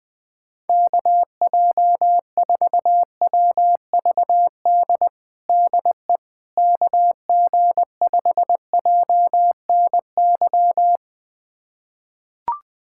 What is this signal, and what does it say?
Morse code, heard as a keyed tone, about 21 wpm, 700 Hz. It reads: KJ4WVD DE KG5JNY
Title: USA Callsign DE Callsign - 6 Characters - 20wpm